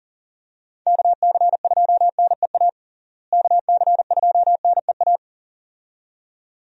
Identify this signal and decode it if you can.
Morse code, heard as a keyed tone, about 40 wpm, 700 Hz. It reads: KC2NEA KC2NEA
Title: USA Call Signs - 6 Characters - Repeated 2x - 40wpm